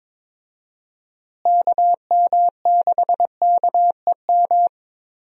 Morse code, heard as a keyed tone, about 22 wpm, 700 Hz. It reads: KM6KEM
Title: USA Call Signs - 6 Characters - Repeated 2x - 22wpm